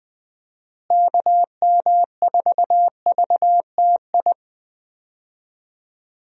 Morse code, heard as a keyed tone, about 21 wpm, 700 Hz. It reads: KM4VTI
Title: USA Call Signs - 6 Characters - Repeated 2x - 20wpm